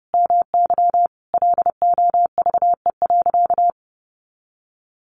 Morse code, heard as a keyed tone, about 30 wpm, 700 Hz. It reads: MY LOVE.
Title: Sentences from Top 1100 Words 30wpm